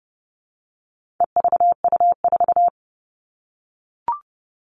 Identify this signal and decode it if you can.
Morse code, heard as a keyed tone, about 30 wpm, 700 Hz. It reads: EVU4
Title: All Cut Numbers - 4-Digit 30wpm